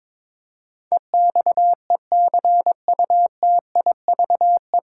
Morse code, heard as a keyed tone, about 22 wpm, 700 Hz. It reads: EXECUTIVE
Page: Morse Code Ninja practice 22wpm